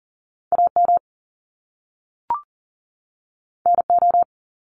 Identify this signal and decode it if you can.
Morse code, heard as a keyed tone, about 42 wpm, 700 Hz. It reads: AM NO
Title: Top Two Letter Combinations in Words 40wpm